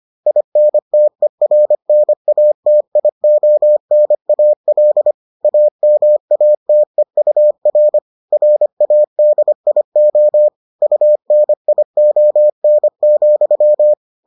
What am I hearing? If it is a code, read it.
Morse code, heard as a keyed tone, about 25 wpm, 600 Hz. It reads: INTERNATIONAL AMATEUR RADIO UNION,